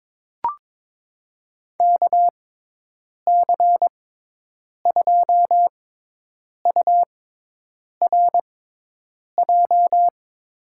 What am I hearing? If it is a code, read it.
Morse code, heard as a keyed tone, about 22 wpm, spaced as for 8 wpm, 700 Hz. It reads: KC2URJ
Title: USA Call Signs - 6x Character Spacing - 22wpm